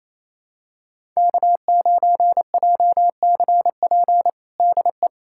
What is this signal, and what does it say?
Morse code, heard as a keyed tone, about 27 wpm, 700 Hz. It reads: K9JCP DE
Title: USA Callsign DE Callsign 28wpm